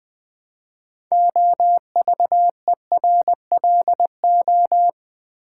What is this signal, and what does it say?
Morse code, heard as a keyed tone, about 20 wpm, 700 Hz. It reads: OVERLO
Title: Build A Word - 2 Letters At A Time - 8-Letter Words - 20wpm